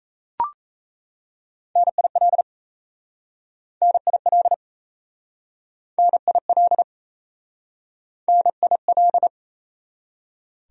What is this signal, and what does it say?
Morse code, heard as a keyed tone, about 37 wpm, 700 Hz. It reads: NIL NIL NIL NIL